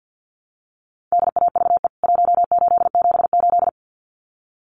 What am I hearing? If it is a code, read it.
Morse code, heard as a keyed tone, about 50 wpm, 700 Hz. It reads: DAVE 1878
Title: CWT Exchange 50wpm